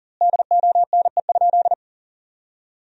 Morse code, heard as a keyed tone, about 40 wpm, 700 Hz. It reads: DONE?
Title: 2x Word-Spacing - Sentences from Top 500 Words - 40wpm